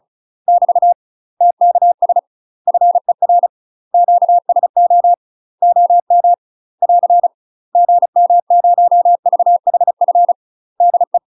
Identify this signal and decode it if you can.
Morse code, heard as a keyed tone, about 34 wpm, 700 Hz. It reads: <BT> TKS FER QSO OM <AR> GM0VHF DE
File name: UK MASTER QSOs TRAINING / 35WPM / UK MASTER QSOs 0037